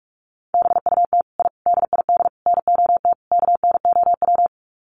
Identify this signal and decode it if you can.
Morse code, heard as a keyed tone, about 44 wpm, 700 Hz. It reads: BUT I DID NOT KNOW
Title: Sets of 5 Words Encore 45wpm